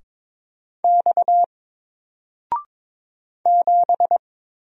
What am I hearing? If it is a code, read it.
Morse code, heard as a keyed tone, about 22 wpm, 700 Hz. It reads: X 7
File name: Single-Letter-Number-22wpm